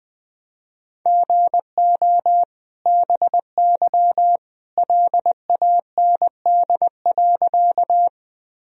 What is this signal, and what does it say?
Morse code, heard as a keyed tone, about 20 wpm, 700 Hz. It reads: GO BY LAND.
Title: Sentences from Top 1000 Words - Review Part 1 - 20wpm